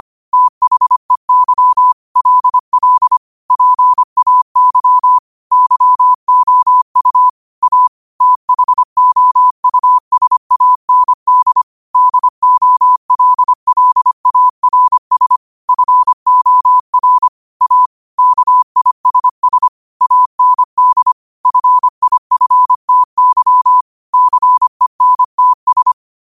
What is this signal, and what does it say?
Morse code, heard as a keyed tone, about 25 wpm, 1000 Hz. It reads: THEY LL PAY YOU A THOUSAND DOLLARS FOR A KISS AND FIFTY CENTS
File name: qotdfiles / 25wpm